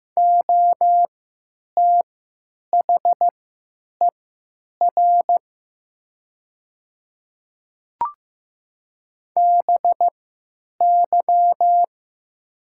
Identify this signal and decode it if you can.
Morse code, heard as a keyed tone, about 15 wpm, spaced as for 9 wpm, 700 Hz. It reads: OTHER BY
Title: 3x Character Spacing - Top 100 Words - 15wpm